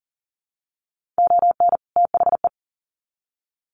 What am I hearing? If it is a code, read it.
Morse code, heard as a keyed tone, about 37 wpm, 700 Hz. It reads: ON THE
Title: Sets of 2 Words 40wpm